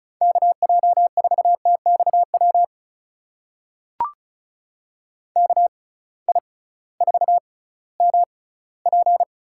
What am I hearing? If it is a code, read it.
Morse code, heard as a keyed tone, about 34 wpm, spaced as for 19 wpm, 700 Hz. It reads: KJ4TXW KI4MP
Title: USA Call Signs - 6 Characters - 6x Character Spacing - 35wpm